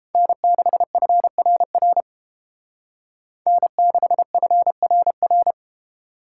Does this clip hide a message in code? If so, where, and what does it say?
Yes: Morse code from 0.1 to 5.5 s, heard as a keyed tone, about 31 wpm, 700 Hz. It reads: N6FRR N6FRR